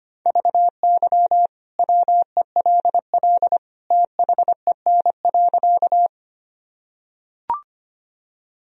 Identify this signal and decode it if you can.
Morse code, heard as a keyed tone, about 25 wpm, 700 Hz. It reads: VY WELL THEN.
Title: Sentences from Top 200 Words and Abbreviations 25wpm